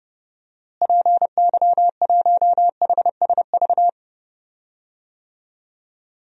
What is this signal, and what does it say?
Morse code, heard as a keyed tone, about 30 wpm, 700 Hz. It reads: PY1HSV
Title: DXCC-100 Call Signs - 6 Characters - Repeated 2x - 30wpm